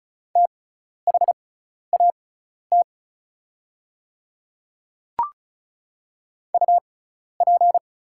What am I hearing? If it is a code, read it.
Morse code, heard as a keyed tone, about 36 wpm, spaced as for 12 wpm, 690 Hz. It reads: THAT UP
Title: 6x Character Spacing - Top 100 Words - 35wpm